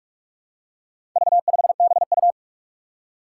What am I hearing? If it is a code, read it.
Morse code, heard as a keyed tone, about 46 wpm, 700 Hz. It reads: V5BU
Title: Four Challenging Letter-Number Combinations 45wpm